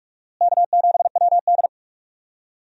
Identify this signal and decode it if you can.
Morse code, heard as a keyed tone, about 46 wpm, 700 Hz. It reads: K7WD
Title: USA Call Signs - 4 Characters - Repeated 3x - 45wpm